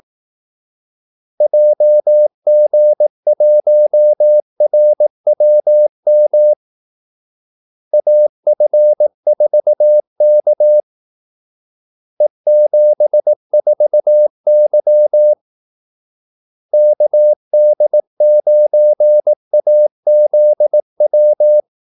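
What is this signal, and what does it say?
Morse code, heard as a keyed tone, about 18 wpm, 600 Hz. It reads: JG1RWM AF4K E74Y KD9AZW
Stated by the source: Morse call-sign practice at 18 wpm.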